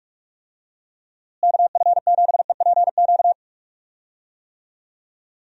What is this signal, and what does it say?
Morse code, heard as a keyed tone, about 46 wpm, 700 Hz. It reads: KF7EPQ
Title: USA Call Signs - 6 Characters - Repeated 2x - 45wpm